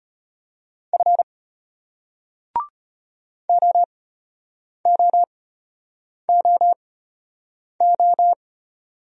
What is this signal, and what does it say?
Morse code, heard as a keyed tone, about 33 wpm, spaced as for 8 wpm, 700 Hz. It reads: F OOOO